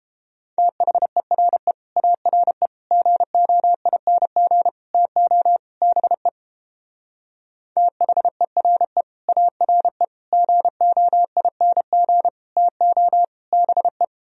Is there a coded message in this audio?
Yes: Morse code from 0.6 to 14.0 s, heard as a keyed tone, about 31 wpm, 700 Hz. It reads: THERE ARE GOING TO BE THERE ARE GOING TO BE